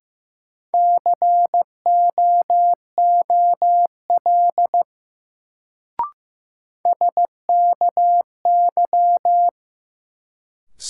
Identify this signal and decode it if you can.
Morse code, heard as a keyed tone, about 15 wpm, 700 Hz. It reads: COOL SKY
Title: Sentences from Top 1200 Words - New Word Drill Alt - 15wpm